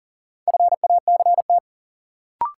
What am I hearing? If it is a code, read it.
Morse code, heard as a keyed tone, about 40 wpm, 700 Hz. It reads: FACT
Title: Sentences from Top 500 Words - New Word Drill - 40wpm